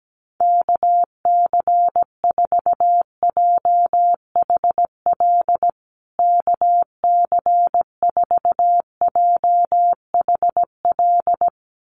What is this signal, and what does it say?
Morse code, heard as a keyed tone, about 17 wpm, 700 Hz. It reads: KC4JHL KC4JHL
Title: USA Call Signs Repeated 2x 17wpm